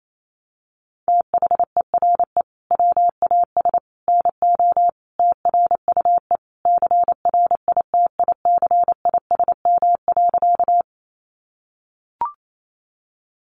Morse code, heard as a keyed tone, about 28 wpm, 700 Hz. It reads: THERE WAS NO TRUE CRITICISM.
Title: Sentences from Top 1300 Words 28wpm